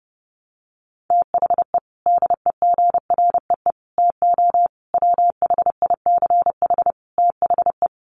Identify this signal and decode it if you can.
Morse code, heard as a keyed tone, about 30 wpm, 700 Hz. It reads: THE DEGREE TO WHICH THE
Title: Sets of 5 Words Encore2 30wpm